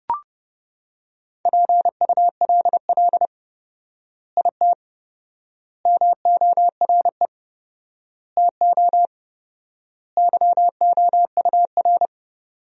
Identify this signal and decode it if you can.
Morse code, heard as a keyed tone, about 30 wpm, 700 Hz. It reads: PULL IT MORE TO YOUR
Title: 4x Word-Spacing - Sentences from Top 400 Words - 30wpm